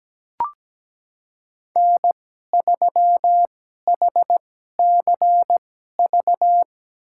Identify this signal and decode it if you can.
Morse code, heard as a keyed tone, about 17 wpm, spaced as for 12 wpm, 700 Hz. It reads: N3HCV
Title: USA Call Signs - 5 Characters - 2x Character Spacing - 17wpm